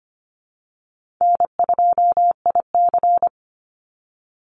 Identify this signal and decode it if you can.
Morse code, heard as a keyed tone, about 25 wpm, 700 Hz. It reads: N2IC
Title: Call Signs Repeated 2x - 25wpm